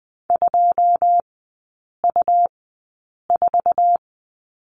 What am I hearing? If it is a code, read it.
Morse code, heard as a keyed tone, about 20 wpm, spaced as for 8 wpm, 700 Hz. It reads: 2U4
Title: 100 Lesson 17e - Letter u - Three Characters 20wpm